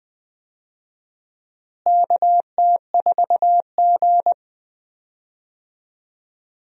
Morse code, heard as a keyed tone, about 20 wpm, 700 Hz. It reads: KT4G
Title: USA Call Signs - 4 Characters - Repeated 3x - 20wpm